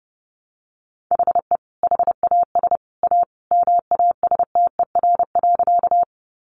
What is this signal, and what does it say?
Morse code, heard as a keyed tone, about 30 wpm, 700 Hz. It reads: HE HAS A MASTER.
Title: Sentences from Top 1300 Words 30wpm